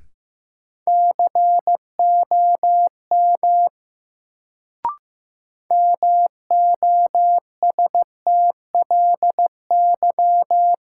Morse code, heard as a keyed tone, about 15 wpm, 700 Hz. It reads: COM MOSTLY
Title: Sentences from Top 1300 Words - New Word Drill Alt - 15wpm